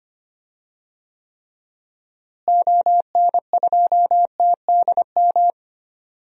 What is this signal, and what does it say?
Morse code, heard as a keyed tone, about 25 wpm, 700 Hz. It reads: ON2TDM
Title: DXCC-100 Call Signs - 6 Characters - Repeated 2x - 25wpm